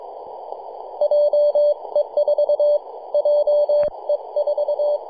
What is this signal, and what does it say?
Morse code, heard as a keyed tone, about 23 wpm, 590 Hz. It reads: JE4 JE4
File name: 3d2ya-18-085-cw